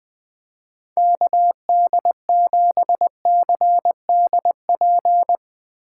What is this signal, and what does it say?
Morse code, heard as a keyed tone, about 19 wpm, 700 Hz. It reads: KD7CDP
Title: Speed Racing - USA 2x3 Call Signs 17wpm